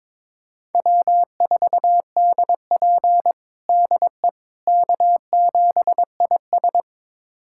Morse code, heard as a keyed tone, about 21 wpm, 700 Hz. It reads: W4DP DE K7IS
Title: USA Callsign DE Callsign - 4 Characters - 22wpm